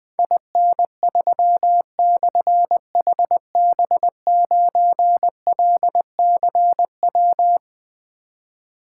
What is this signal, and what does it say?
Morse code, heard as a keyed tone, about 20 wpm, 700 Hz. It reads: IN3/HB9LCW
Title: Difficult Call Signs 20wpm